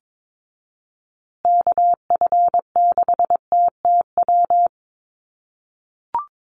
Morse code, heard as a keyed tone, about 22 wpm, 700 Hz. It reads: KF6TTW
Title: USA Call Signs Repeated 3x 22wpm